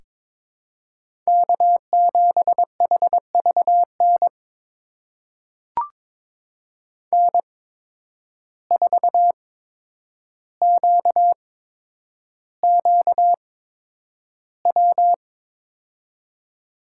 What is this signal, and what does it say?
Morse code, heard as a keyed tone, about 22 wpm, spaced as for 9 wpm, 700 Hz. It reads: K7HVN N4QQW
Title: USA Call Signs - 5 Characters - 8x Character Spacing - 22wpm